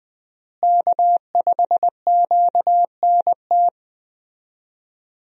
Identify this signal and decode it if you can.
Morse code, heard as a keyed tone, about 20 wpm, 700 Hz. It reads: K5QNT
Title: USA Call Signs - 5 Characters - Repeated 2x - 20wpm